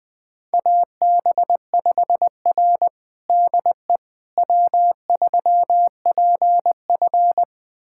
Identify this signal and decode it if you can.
Morse code, heard as a keyed tone, about 20 wpm, 700 Hz. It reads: AB5R DE W3PF
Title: USA Callsign DE Callsign - 4 Characters - 20wpm